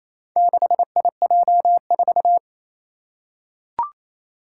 Morse code, heard as a keyed tone, about 29 wpm, 700 Hz. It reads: 6IJ4
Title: Four Challenging Letter-Number Combinations 28wpm